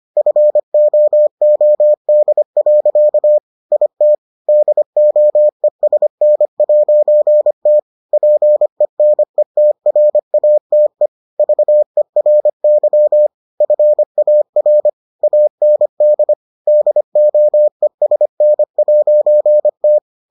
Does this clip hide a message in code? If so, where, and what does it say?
Yes: Morse code from 0.2 to 20.0 s, heard as a keyed tone, about 25 wpm, 600 Hz. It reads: FOOD. IT DOESN'T PENETRATE VERY FAR AND DOESN'T